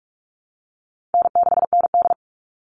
Morse code, heard as a keyed tone, about 44 wpm, 700 Hz. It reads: N6ND
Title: All Cut Numbers - 4-Digit 45wpm